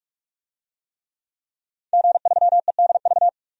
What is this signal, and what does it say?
Morse code, heard as a keyed tone, about 46 wpm, 700 Hz. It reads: G3EDV
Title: DXCC-100 Call Signs - 5 Characters - 45wpm